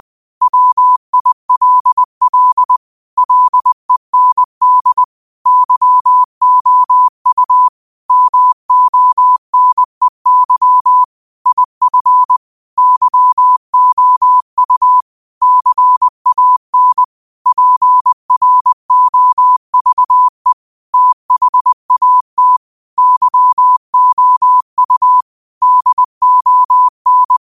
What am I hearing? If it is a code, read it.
Morse code, heard as a keyed tone, about 20 wpm, 1000 Hz. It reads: WILL LEND YOU MONEY IF YOU CAN PROVE THAT YOU DON
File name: qotdfiles / 20wpm